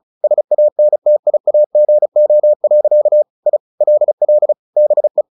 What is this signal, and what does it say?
Morse code, heard as a keyed tone, about 34 wpm, 600 Hz. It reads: SANTIAGO. I LL BE
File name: the old man and the sea 2 35wpm